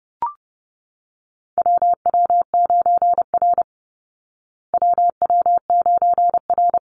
Morse code, heard as a keyed tone, about 30 wpm, 700 Hz. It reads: WW9R WW9R
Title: Call Signs Repeated 3x - 30wpm